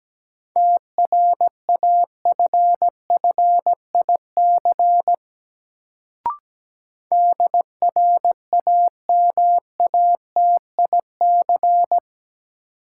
Morse code, heard as a keyed tone, about 17 wpm, 700 Hz. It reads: TRAFFIC DRAMATIC
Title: Sentences from Top 1300 Words - New Word Drill - 17wpm